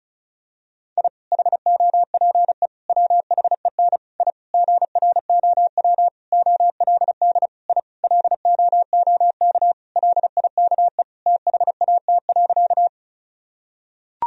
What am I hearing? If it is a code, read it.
Morse code, heard as a keyed tone, about 34 wpm, 700 Hz. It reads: I HOPE WHEN I GROW OLD I LOOK LIKE THAT.
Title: Sentences from Top 1000 Words - Review Part 1 - 35wpm